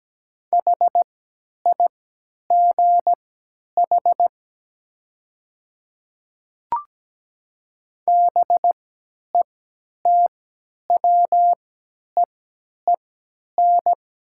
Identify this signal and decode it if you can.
Morse code, heard as a keyed tone, about 17 wpm, spaced as for 10 wpm, 690 Hz. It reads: HIGH BETWEEN
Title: 3x Character Spacing - Top 300 Words - 17wpm